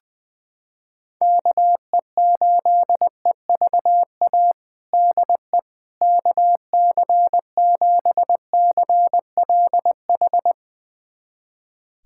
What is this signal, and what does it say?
Morse code, heard as a keyed tone, about 20 wpm, 700 Hz. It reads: KE8EVA DE KC7CLH
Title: USA Callsign DE Callsign 20wpm